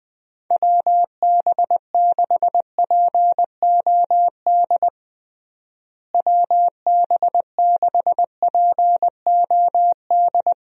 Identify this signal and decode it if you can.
Morse code, heard as a keyed tone, about 20 wpm, 700 Hz. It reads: WB6POD WB6POD
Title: USA Call Signs - 6 Characters - Repeated 3x - 20wpm